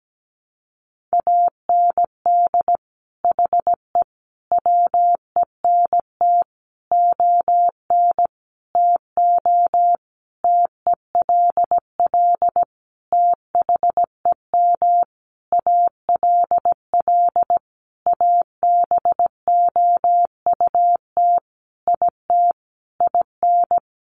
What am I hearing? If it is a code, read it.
Morse code, heard as a keyed tone, about 17 wpm, 700 Hz. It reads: AND HE WENT ON TO TELL THEM ALL ABOUT IT IN
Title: Sentences from Top 900 Words 17wpm